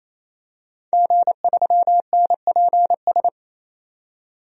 Morse code, heard as a keyed tone, about 28 wpm, 700 Hz. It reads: G3NPS
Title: DXCC-100 Call Signs - 5 Characters - 28wpm